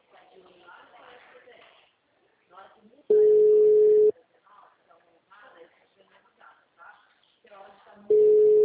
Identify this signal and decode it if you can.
Ringback tone (Europe, 425 Hz), with speech.
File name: out- -2011- - - .78856